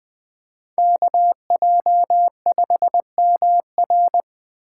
Morse code, heard as a keyed tone, about 20 wpm, 700 Hz. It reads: KJ5MR
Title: Speed Racing - USA 2x2 Call Signs 17wpm